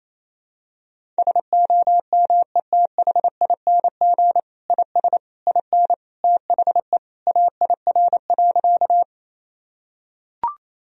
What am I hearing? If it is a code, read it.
Morse code, heard as a keyed tone, about 27 wpm, 700 Hz. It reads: SOMETHING IS IN THE AIR.
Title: Sentences from Top 1000 Words - Review Part 1 - 28wpm